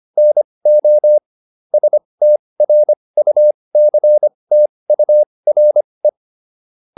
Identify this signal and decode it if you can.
Morse code, heard as a keyed tone, about 25 wpm, spaced as for 20 wpm, 600 Hz. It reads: NO STRUCTURE
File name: SS306 20